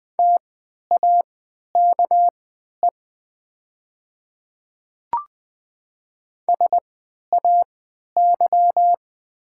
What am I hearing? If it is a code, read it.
Morse code, heard as a keyed tone, about 20 wpm, spaced as for 11 wpm, 700 Hz. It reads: TAKE SAY
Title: 3x Character Spacing - Top 500 Words - 20wpm